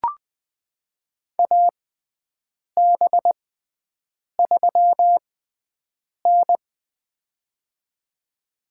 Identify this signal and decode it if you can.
Morse code, heard as a keyed tone, about 20 wpm, spaced as for 7 wpm, 700 Hz. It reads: AB3N
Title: USA Call Signs - 6x Character Spacing - 20wpm